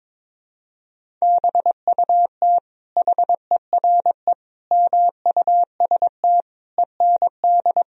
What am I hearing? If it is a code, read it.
Morse code, heard as a keyed tone, about 22 wpm, 700 Hz. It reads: BUT HERE MUST END
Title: Sentences from Top 1000 Words - Review Part 1 - 22wpm